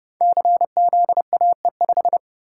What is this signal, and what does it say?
Morse code, heard as a keyed tone, about 29 wpm, 700 Hz. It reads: CZAE5
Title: Code Groups - 5 Letter:Numbers - 30wpm